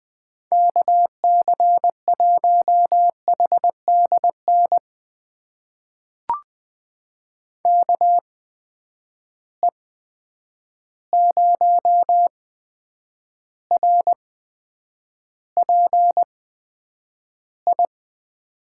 Morse code, heard as a keyed tone, about 20 wpm, spaced as for 9 wpm, 700 Hz. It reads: KC1HDN KE0RPI
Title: USA Call Signs - 8x Character Spacing - 20wpm